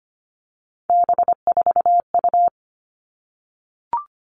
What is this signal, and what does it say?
Morse code, heard as a keyed tone, about 25 wpm, 700 Hz. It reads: B4U
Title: All Cut Numbers - 3-Digit 25wpm